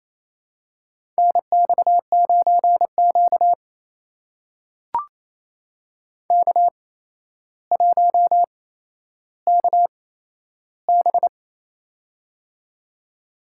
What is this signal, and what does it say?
Morse code, heard as a keyed tone, about 28 wpm, spaced as for 12 wpm, 700 Hz. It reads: NX9Q K1KB
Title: USA Call Signs - 4 Characters - 8x Character Spacing - 28wpm